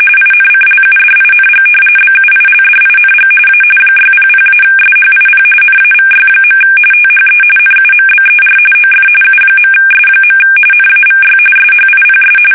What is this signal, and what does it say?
RTTY, about 45.45 baud. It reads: INCLUDING BUILDING A 600-SHIP NAVY. THE NUMBER OF U.S. NAVY SHIPS
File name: fsk850 / battleshipnj